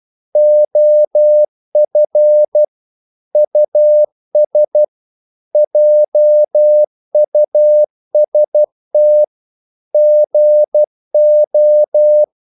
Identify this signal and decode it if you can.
Morse code, heard as a keyed tone, about 12 wpm, 600 Hz. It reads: OF US JUST GO
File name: CW / Practice / EnglishQuotations0112